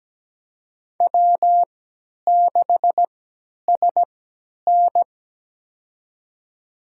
Morse code, heard as a keyed tone, about 17 wpm, spaced as for 10 wpm, 700 Hz. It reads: W6SN
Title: USA Call Signs - 3x Character Spacing - 17wpm